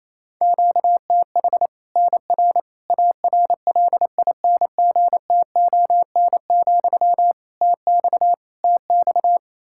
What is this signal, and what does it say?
Morse code, heard as a keyed tone, about 28 wpm, 700 Hz. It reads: QTH NR ARLINGTON, TX TX
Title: K1USN SST QSO Practice 28wpm